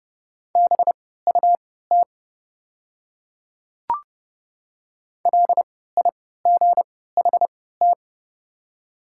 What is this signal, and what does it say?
Morse code, heard as a keyed tone, about 30 wpm, spaced as for 17 wpm, 690 Hz. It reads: BUT LIGHT
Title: 3x Character Spacing - Top 200 Words - 30wpm